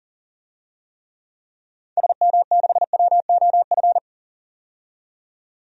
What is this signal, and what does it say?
Morse code, heard as a keyed tone, about 40 wpm, 700 Hz. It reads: SM6WOF
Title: DXCC-100 Call Signs - 6 Characters - 40wpm